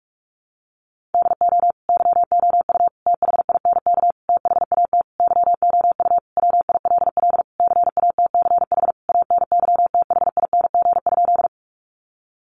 Morse code, heard as a keyed tone, about 44 wpm, 700 Hz. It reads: DO YOU THINK THAT YOU WILL CATCH ANYTHING?
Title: Sentences from Top 1500 Words 45wpm